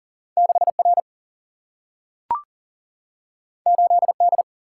Morse code, heard as a keyed tone, about 40 wpm, 700 Hz. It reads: 6R 8D
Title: Single-Letter Single-Number 40wpm